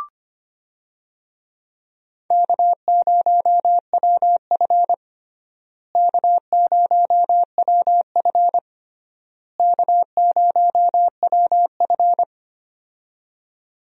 Morse code, heard as a keyed tone, about 25 wpm, 700 Hz. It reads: K0WF K0WF K0WF
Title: USA Call Signs - 4 Characters - Repeated 3x - 25wpm